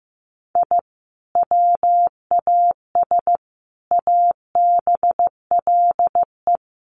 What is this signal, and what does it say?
Morse code, heard as a keyed tone, about 15 wpm, 700 Hz. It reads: I WAS ABLE
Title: Sets of 4 Words 15wpm